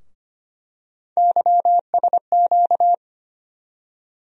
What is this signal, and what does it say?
Morse code, heard as a keyed tone, about 25 wpm, 700 Hz. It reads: YSQ
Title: USA Call Sign Postfixes - 3 Letters 25wpm